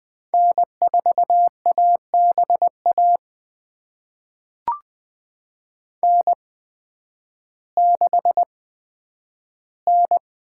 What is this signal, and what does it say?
Morse code, heard as a keyed tone, about 20 wpm, spaced as for 11 wpm, 700 Hz. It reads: N4ABA N6N
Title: USA Call Signs - 5 Characters - 8x Character Spacing - 20wpm